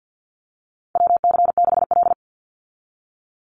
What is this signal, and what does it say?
Morse code, heard as a keyed tone, about 50 wpm, 700 Hz. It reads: WC6L
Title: USA Call Signs 50wpm